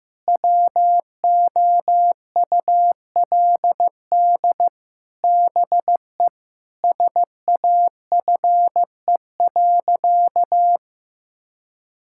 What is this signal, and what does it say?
Morse code, heard as a keyed tone, about 16 wpm, 700 Hz. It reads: WOULD BE SAFE.
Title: Sentences from Top 1600 Words 15wpm